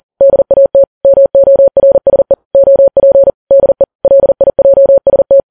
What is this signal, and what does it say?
Morse code, heard as a keyed tone, about 40 wpm, 550 Hz. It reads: DAT MORSE OP DE LIJST
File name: Files Web tls / Morse on9fv / morse40kps